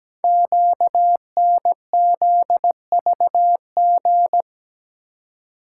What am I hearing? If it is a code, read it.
Morse code, heard as a keyed tone, about 17 wpm, 700 Hz. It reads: QNZVG
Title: Code Groups - 5 Letters - 17wpm